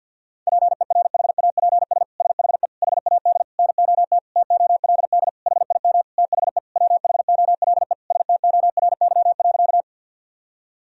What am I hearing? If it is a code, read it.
Morse code, heard as a keyed tone, about 49 wpm, 700 Hz. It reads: PERHAPS SHE HAD NOT TOLD HIM THE WHOLE STORY.
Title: Sentences from Top 1000 Words - Review Part 1 - 50wpm